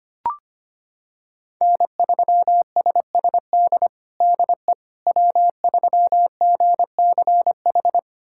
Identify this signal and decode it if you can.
Morse code, heard as a keyed tone, about 25 wpm, 700 Hz. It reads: N3SSD DE W3GCH
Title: USA Callsign DE Callsign - 5 Characters - 25wpm